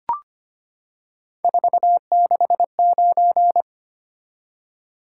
Morse code, heard as a keyed tone, about 25 wpm, 700 Hz. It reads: 469
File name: Three Letter-Number Combinations - 25wpm